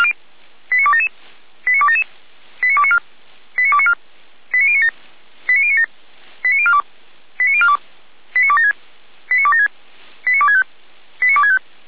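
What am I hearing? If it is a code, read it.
Five-tone selective call (ZVEI), heard as quick runs of five tones, tones of about 70 ms. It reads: 89277 89277 89294 89294 89097 89097 89042 89042 89276 89276 89265 89265
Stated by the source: ZVEI five-tone call received by SDR.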